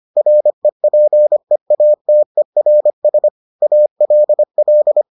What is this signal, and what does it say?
Morse code, heard as a keyed tone, about 25 wpm, 600 Hz. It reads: REPEATERS ALL